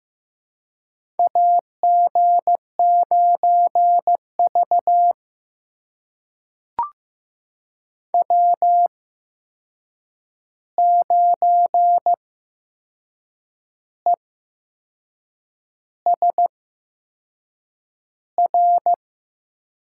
Morse code, heard as a keyed tone, about 15 wpm, spaced as for 6 wpm, 700 Hz. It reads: AG9V W9ESR
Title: USA Call Signs - 8x Character Spacing - 15wpm